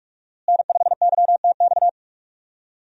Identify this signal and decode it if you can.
Morse code, heard as a keyed tone, about 42 wpm, 700 Hz. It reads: N5YTX
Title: USA Call Signs - 5 Characters - Repeated 3x - 45wpm